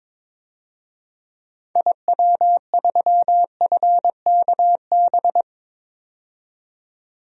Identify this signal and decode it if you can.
Morse code, heard as a keyed tone, about 22 wpm, 700 Hz. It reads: IW3FKB
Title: DXCC-100 Call Signs - 6 Characters - 22wpm